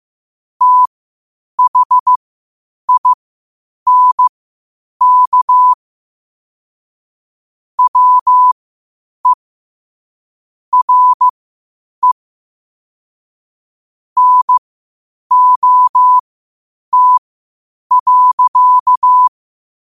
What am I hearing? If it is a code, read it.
Morse code, heard as a keyed tone, about 15 wpm, spaced as for 8 wpm, 1000 Hz. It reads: THINK WE RE NOT.
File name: qotdfiles / 8wpm